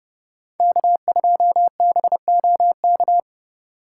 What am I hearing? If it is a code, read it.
Morse code, heard as a keyed tone, about 33 wpm, 700 Hz. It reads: K2BOK